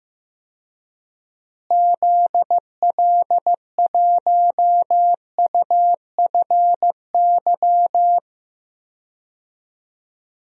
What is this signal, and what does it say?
Morse code, heard as a keyed tone, about 15 wpm, 700 Hz. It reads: ZL1UFY
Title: DXCC-100 Call Signs - 6 Characters - Repeated 2x - 15wpm